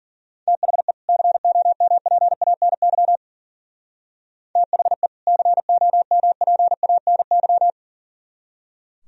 Morse code, heard as a keyed tone, about 48 wpm, 700 Hz. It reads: THE COMPANY THE COMPANY
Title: Speed-Racing - Sets of 2 Words 40wpm